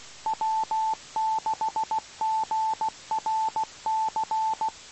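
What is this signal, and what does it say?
Morse code, heard as a keyed tone, about 16 wpm, 880 Hz. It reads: W6GRC
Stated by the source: Morse code received on 2 m at 147.105 MHz.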